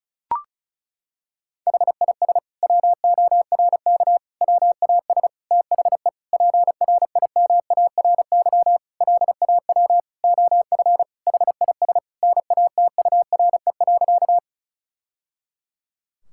Morse code, heard as a keyed tone, about 34 wpm, 700 Hz. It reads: HIS WORK WAS THE PRIMARY LAW OF HIS NATURE.
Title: Sentences from Top 1000 Words - Review Part 5 - 35wpm